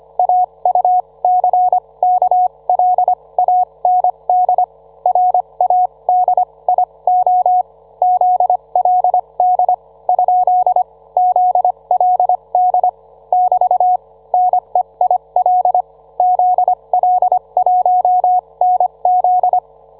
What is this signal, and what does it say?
Morse code, heard as a keyed tone, about 25 wpm, spaced as for 21 wpm, 730 Hz. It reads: AUCKLAND RADIO ZLD ? ZLD <BT> NEIL ZL1NZ